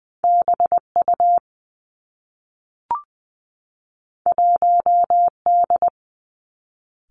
Morse code, heard as a keyed tone, about 20 wpm, 700 Hz. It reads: B U 1 D
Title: Two Challenging Letter-Number Combinations 20wpm